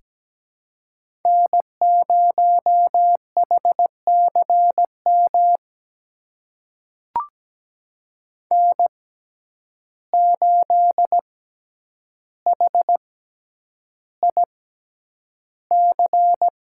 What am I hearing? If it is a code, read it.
Morse code, heard as a keyed tone, about 17 wpm, spaced as for 9 wpm, 700 Hz. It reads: N0HCM N8HIC
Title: USA Call Signs - 5 Characters - 6x Character Spacing - 17wpm